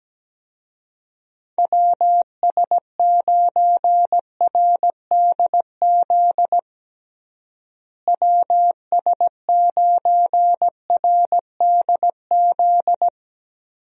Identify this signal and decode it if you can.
Morse code, heard as a keyed tone, about 17 wpm, 700 Hz. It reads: WS9RDZ WS9RDZ
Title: USA Call Signs - 6 Characters - Repeated 2x - 17wpm